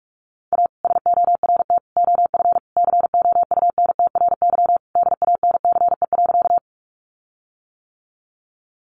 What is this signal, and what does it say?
Morse code, heard as a keyed tone, about 44 wpm, 700 Hz. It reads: A SORT OF COUNTRY DANCE.
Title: Sentences from Top 1300 Words 45wpm